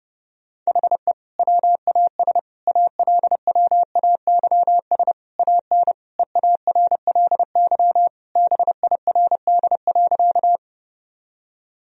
Morse code, heard as a keyed tone, about 30 wpm, 700 Hz. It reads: HE WAS ALWAYS AN EARLY BIRD.
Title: Sentences from Top 1000 Words - Review Part 2 - 30wpm